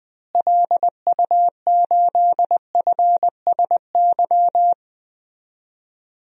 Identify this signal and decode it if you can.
Morse code, heard as a keyed tone, about 21 wpm, 700 Hz. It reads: LU8FSY
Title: DXCC-100 Call Signs - 6 Characters - Repeated 2x - 20wpm